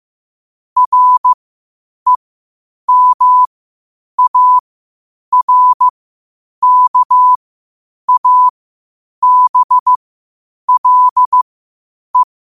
Morse code, heard as a keyed tone, about 15 wpm, spaced as for 8 wpm, 1000 Hz. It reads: REMARKABLE
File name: qotdfiles / 8wpm